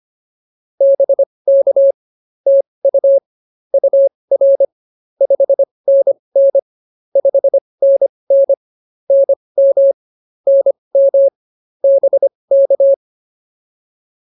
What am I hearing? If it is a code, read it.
Morse code, heard as a keyed tone, about 24 wpm, spaced as for 20 wpm, 550 Hz. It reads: BK TU UR 5NN 5NN NM NM BK